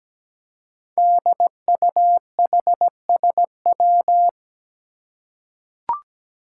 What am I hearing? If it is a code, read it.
Morse code, heard as a keyed tone, about 17 wpm, 700 Hz. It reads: DUHSW
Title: Five Challenging Letter-Number Combinations 17wpm